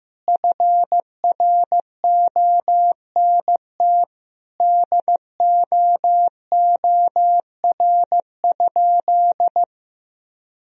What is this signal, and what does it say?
Morse code, heard as a keyed tone, about 16 wpm, 700 Hz. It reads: FRONT DOOR?
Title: Sentences from Top 1000 Words - Review Part 2 - 15wpm